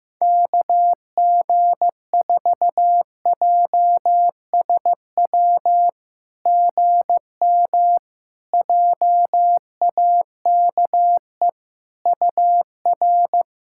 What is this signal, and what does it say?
Morse code, heard as a keyed tone, about 15 wpm, 700 Hz. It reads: KG4JSW GM JAKE UR
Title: SOTA QSO Practice 15wpm